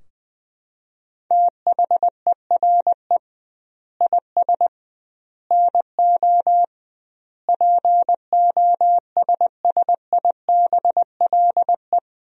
Morse code, heard as a keyed tone, about 20 wpm, 700 Hz. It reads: THERE IS NO POSSIBLE
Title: 2x Word-Spacing - Sentences from Top 500 Words - 20wpm